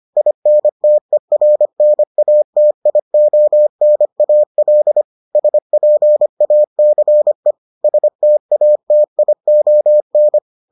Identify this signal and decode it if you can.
Morse code, heard as a keyed tone, about 25 wpm, 600 Hz. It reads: INTERNATIONAL SPACE STATION